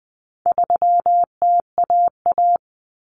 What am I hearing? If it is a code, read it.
Morse code, heard as a keyed tone, about 20 wpm, 700 Hz. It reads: 3TAA
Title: Common Cut Numbers - 4-Digit 20wpm